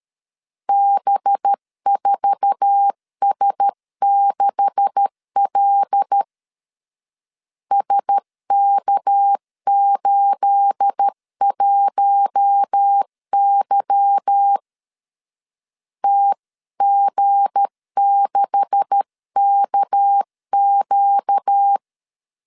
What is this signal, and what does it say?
Morse code, heard as a keyed tone, about 13 wpm, 800 Hz. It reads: B4S6L SK81Y TG6KQ
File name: LEZIONI / 9 286 12wpm 2305